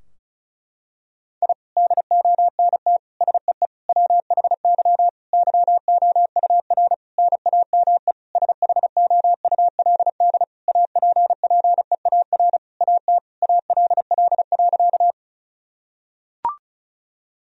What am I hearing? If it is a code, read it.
Morse code, heard as a keyed tone, about 36 wpm, 700 Hz. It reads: I DONT SEE WHY YOUR NAME SHOULD APPEAR AT ALL.
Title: Sentences from Top 1000 Words - Review Part 2 - 35wpm